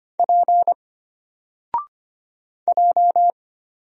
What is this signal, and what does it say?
Morse code, heard as a keyed tone, about 25 wpm, 700 Hz. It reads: P J
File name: Single-Letter-Number-25wpm